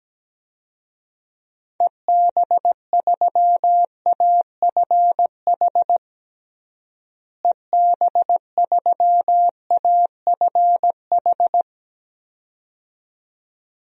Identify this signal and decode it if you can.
Morse code, heard as a keyed tone, about 17 wpm, 700 Hz. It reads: EB3AFH EB3AFH
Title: DXCC-100 Call Signs - 6 Characters - Repeated 2x - 17wpm